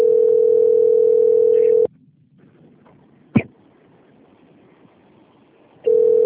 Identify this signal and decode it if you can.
Ringback tone (North America), 440 + 480 Hz.
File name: out- - - - - .4679